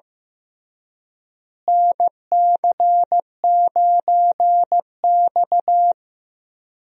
Morse code, heard as a keyed tone, about 15 wpm, 700 Hz. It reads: NC9X
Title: USA Call Signs - 4 Characters - Repeated 3x - 15wpm